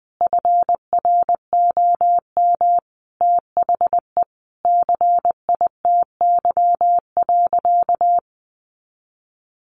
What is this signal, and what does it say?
Morse code, heard as a keyed tone, about 19 wpm, 700 Hz. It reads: FROM THE CITY.
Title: Sentences from Top 1500 Words 20wpm